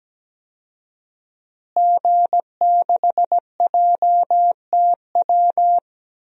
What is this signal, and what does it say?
Morse code, heard as a keyed tone, about 17 wpm, 700 Hz. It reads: G6JTW
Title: DXCC-100 Call Signs - 5 Characters - Repeated 3x - 17wpm